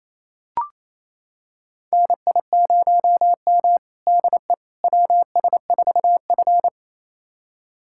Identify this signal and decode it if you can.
Morse code, heard as a keyed tone, about 28 wpm, 700 Hz. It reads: NI0M DE WS4F
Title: USA Callsign DE Callsign - 4 Characters - 28wpm